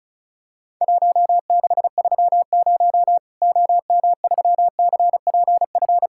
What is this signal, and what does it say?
Morse code, heard as a keyed tone, about 36 wpm, 700 Hz. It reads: 1630 OM3CPF
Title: Worked All Europe DX Contest - QTC Exchange - 35wpm